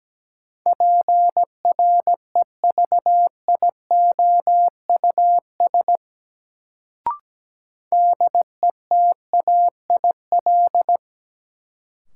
Morse code, heard as a keyed tone, about 17 wpm, 700 Hz. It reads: PREVIOUS DETAIL
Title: Sentences from Top 1500 Words - New Word Drill - 17wpm